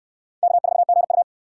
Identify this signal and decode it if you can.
Morse code, heard as a keyed tone, about 58 wpm, 700 Hz. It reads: K4GU